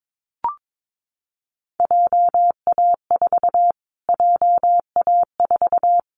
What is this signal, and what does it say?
Morse code, heard as a keyed tone, about 22 wpm, 700 Hz. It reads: JA4 JA4
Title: Callsign Prefixes for Top 25 DXCC - Repeated 3x - 22wpm